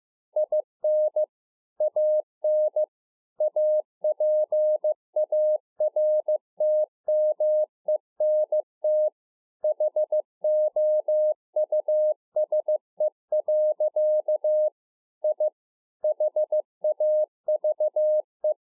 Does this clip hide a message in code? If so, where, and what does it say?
Yes: Morse code from 0.4 to 18.5 s, heard as a keyed tone, about 15 wpm, 610 Hz. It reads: IN AN APARTMENT HOUSE. I HAVE